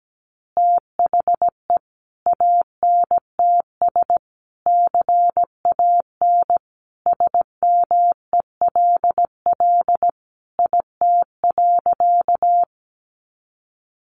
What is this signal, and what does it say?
Morse code, heard as a keyed tone, about 17 wpm, 700 Hz. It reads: THE ANTS CAN SMELL IT.
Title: Additional Easy Phrases 17wpm